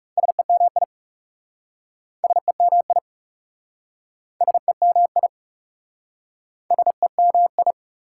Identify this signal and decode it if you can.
Morse code, heard as a keyed tone, about 37 wpm, 700 Hz. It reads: SEMI SEMI SEMI SEMI
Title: Speed-Racing Prefixes and Suffixes 30wpm